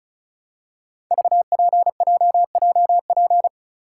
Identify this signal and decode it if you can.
Morse code, heard as a keyed tone, about 36 wpm, 700 Hz. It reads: VPJJP
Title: Code Groups - 5 Letters - 35wpm